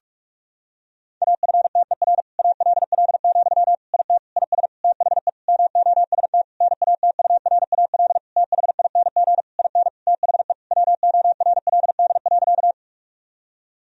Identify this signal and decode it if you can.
Morse code, heard as a keyed tone, about 46 wpm, 700 Hz. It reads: AFTER ALL, IT IS THE MOST NATURAL THING IN THE WORLD.
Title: Sentences from Top 1000 Words - Review Part 4 - 45wpm